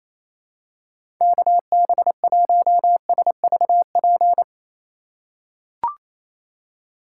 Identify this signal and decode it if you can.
Morse code, heard as a keyed tone, about 28 wpm, 700 Hz. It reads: KB1SVP
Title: USA Call Signs - 6 Characters - 2x Character Spacing - 28wpm